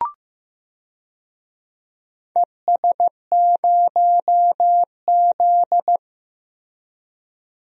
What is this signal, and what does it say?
Morse code, heard as a keyed tone, about 15 wpm, 700 Hz. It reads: ES0Z
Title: DXCC-100 Call Signs - 4 Characters - Repeated 2x - 15wpm